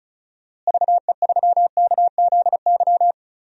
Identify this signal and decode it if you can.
Morse code, heard as a keyed tone, about 36 wpm, 700 Hz. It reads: VE3KZY
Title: DXCC-100 Call Signs - 6 Characters - Repeated 2x - 35wpm